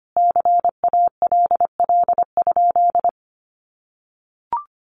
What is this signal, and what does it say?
Morse code, heard as a keyed tone, about 25 wpm, 700 Hz. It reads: CALL?
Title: Sentences from Top 100 Words 25wpm